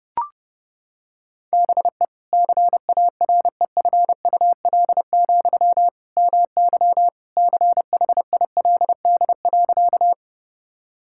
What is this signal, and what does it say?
Morse code, heard as a keyed tone, about 32 wpm, 700 Hz. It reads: BE CAREFUL, MY CHILD.
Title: Sentences from Top 1600 Words 30wpm